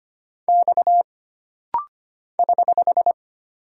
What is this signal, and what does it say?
Morse code, heard as a keyed tone, about 25 wpm, 700 Hz. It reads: X <HH>
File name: Single Character (Letter:Number:Punctuation:Prosign) - Rapid-Fire - 25wpm